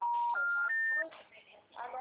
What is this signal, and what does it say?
Special information tone, with speech.